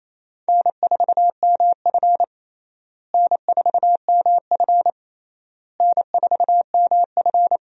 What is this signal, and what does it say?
Morse code, heard as a keyed tone, about 28 wpm, 700 Hz. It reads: N4MF N4MF N4MF
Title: USA Call Signs - 4 Characters - Repeated 3x - 28wpm